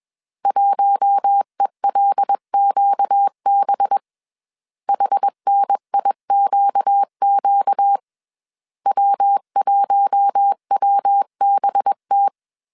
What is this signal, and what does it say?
Morse code, heard as a keyed tone, about 21 wpm, 800 Hz. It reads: 1ELQB HNIQQ W1WBT
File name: CW / DET / 1 6 1500C S C 20wpm 2116